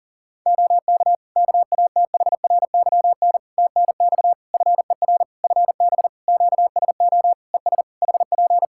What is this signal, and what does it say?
Morse code, heard as a keyed tone, about 40 wpm, 700 Hz. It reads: OK KATHRYN TNX FER FB QSO ES HP
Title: K1USN SST QSO Practice 40wpm